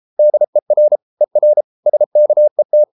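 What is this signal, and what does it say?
Morse code, heard as a keyed tone, about 33 wpm, 600 Hz. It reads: DER ER SKET
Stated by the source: Morse code at 33 wpm, QRQ.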